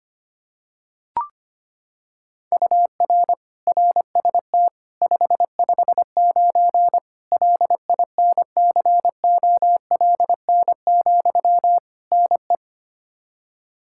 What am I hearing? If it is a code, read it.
Morse code, heard as a keyed tone, about 25 wpm, 700 Hz. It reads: UR RST 559 LINCOLN, NE
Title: Practice QSO - Exchange 1 - Beginning - 25wpm